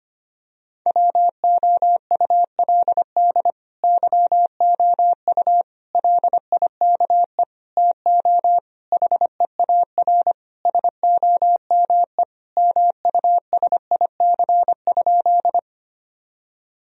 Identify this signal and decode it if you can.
Morse code, heard as a keyed tone, about 25 wpm, 700 Hz. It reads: WOULD YOU LIKE TO HEAR SOME MUSIC?
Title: Sentences from Top 1000 Words - Review Part 1 - 25wpm